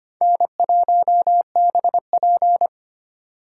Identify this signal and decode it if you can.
Morse code, heard as a keyed tone, about 33 wpm, 700 Hz. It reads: N1BP